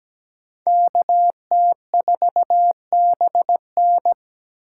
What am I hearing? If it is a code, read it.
Morse code, heard as a keyed tone, about 20 wpm, 700 Hz. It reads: KT4BN
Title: Speed Racing - USA Call Signs 17wpm